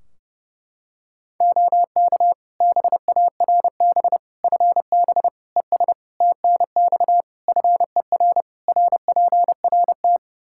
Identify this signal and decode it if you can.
Morse code, heard as a keyed tone, about 32 wpm, 700 Hz. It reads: OK BARB FB ES TNX FER RPRT
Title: Practice QSO - Exchange 2 - Info - 30wpm